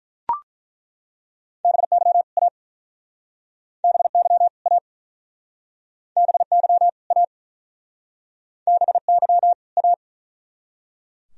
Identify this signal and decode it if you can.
Morse code, heard as a keyed tone, about 50 wpm, 700 Hz. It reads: BY A BY A BY A BY A